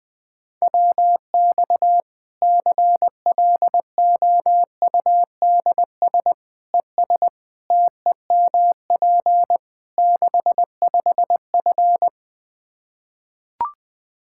Morse code, heard as a keyed tone, about 20 wpm, 700 Hz. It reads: WX CLOUDS ES TEMP 65F
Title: Practice QSO - Exchange 2 - Info - 20wpm